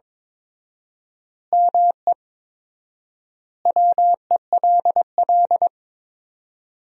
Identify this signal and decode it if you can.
Morse code, heard as a keyed tone, about 22 wpm, 700 Hz. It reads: ME WELL
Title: 4x Word-Spacing - Sentences from Top 400 Words - 22wpm